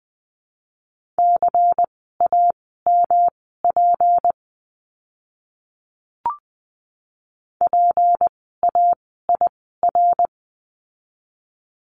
Morse code, heard as a keyed tone, about 20 wpm, spaced as for 15 wpm, 700 Hz. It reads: CAMP PAIR
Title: 153 Lesson 25f - Letter p - Words 20wpm